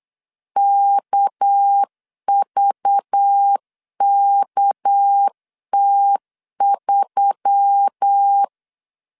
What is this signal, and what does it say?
Morse code, heard as a keyed tone, about 8 wpm, 800 Hz. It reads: KVKT3
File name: LEZIONI / 6 PKX0 / 6 PK 8wpm 3003